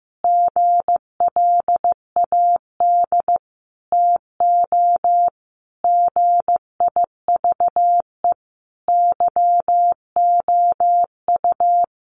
Morse code, heard as a keyed tone, about 15 wpm, 700 Hz. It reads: GLAD TO GIVE YOU
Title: Sentences from Top 800 Words 15wpm